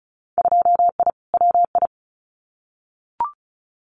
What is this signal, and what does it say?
Morse code, heard as a keyed tone, about 34 wpm, 700 Hz. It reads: 2I WI
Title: Winter Field Day Exchanges 35wpm